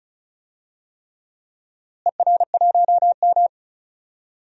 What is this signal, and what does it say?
Morse code, heard as a keyed tone, about 36 wpm, 700 Hz. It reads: ER1M
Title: DXCC-100 Call Signs - 4 Characters - Repeated 3x - 35wpm